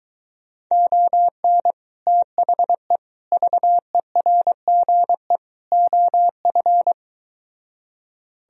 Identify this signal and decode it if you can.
Morse code, heard as a keyed tone, about 23 wpm, 700 Hz. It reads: ON THE VERGE OF